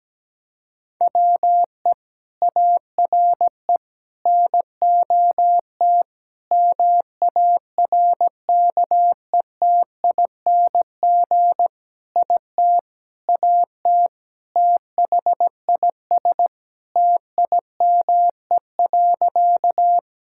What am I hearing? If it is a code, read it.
Morse code, heard as a keyed tone, about 17 wpm, 700 Hz. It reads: WE ARE NOT MARKETING IT AT THIS TIME.
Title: Sentences from Top 1000 Words - Review Part 5 - 17wpm